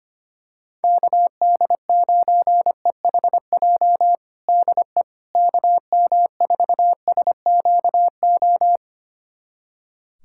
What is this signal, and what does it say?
Morse code, heard as a keyed tone, about 25 wpm, 700 Hz. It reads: KD9EHJ DE KM4SQO
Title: USA Callsign DE Callsign 25wpm